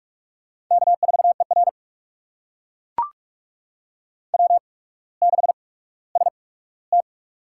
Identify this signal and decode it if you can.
Morse code, heard as a keyed tone, about 44 wpm, spaced as for 19 wpm, 700 Hz. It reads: K4ER W6ST
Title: USA Call Signs - 4 Characters - 8x Character Spacing - 45wpm